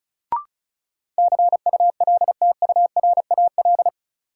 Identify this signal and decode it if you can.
Morse code, heard as a keyed tone, about 36 wpm, 700 Hz. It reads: CULTURAL
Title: Sentences from Top 1500 Words - New Word Drill Alt - 35wpm